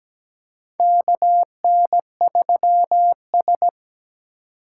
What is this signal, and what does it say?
Morse code, heard as a keyed tone, about 17 wpm, 700 Hz. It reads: KN3S
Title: USA Call Signs - 4 Characters - 2x Character Spacing - 17wpm